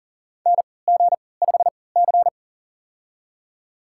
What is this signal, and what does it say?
Morse code, heard as a keyed tone, about 40 wpm, spaced as for 23 wpm, 700 Hz. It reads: NG5C
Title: USA Call Signs - 4 Characters - 3x Character Spacing - 40wpm